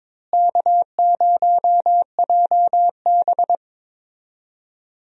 Morse code, heard as a keyed tone, about 22 wpm, 700 Hz. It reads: K0JB
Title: USA Call Signs - 4 Characters - Repeated 3x - 22wpm